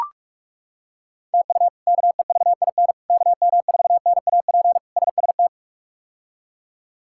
Morse code, heard as a keyed tone, about 41 wpm, 700 Hz. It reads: TU KEVIN KM4NAP SST
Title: K1USN SST QSO Practice - 45wpm